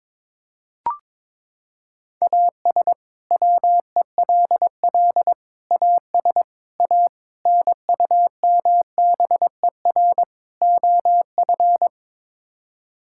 Morse code, heard as a keyed tone, about 22 wpm, 700 Hz. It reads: AS WELL AS A NUMBER OF
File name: practice / 22wpm / Sets of 6 Words - 22wpm